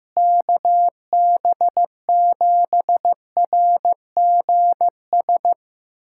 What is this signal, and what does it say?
Morse code, heard as a keyed tone, about 15 wpm, 700 Hz. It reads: KB7RGS
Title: USA Callsign DE Callsign - 6 Characters - 15wpm